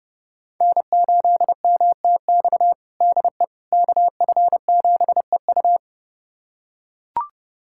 Morse code, heard as a keyed tone, about 30 wpm, 700 Hz. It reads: N8MTX DE KF7EU
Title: USA Callsign DE Callsign - 5 Characters - 30wpm